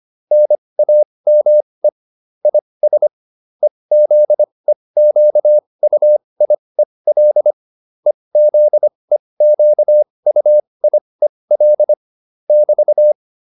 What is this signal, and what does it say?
Morse code, heard as a keyed tone, about 25 wpm, spaced as for 20 wpm, 600 Hz. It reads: NAME IS EZEQUIEL EZEQUIEL <BT>